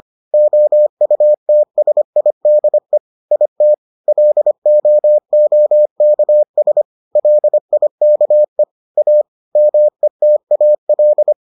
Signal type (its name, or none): Morse code